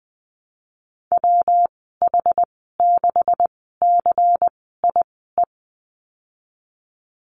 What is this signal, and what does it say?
Morse code, heard as a keyed tone, about 20 wpm, spaced as for 15 wpm, 700 Hz. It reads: WH6CIE
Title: 200 Lesson 32g - Letter k - Call Signs 20wpm